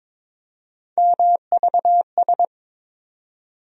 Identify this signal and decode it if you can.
Morse code, heard as a keyed tone, about 22 wpm, 700 Hz. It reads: MVS
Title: USA Call Sign Postfixes - 3 Letters 22wpm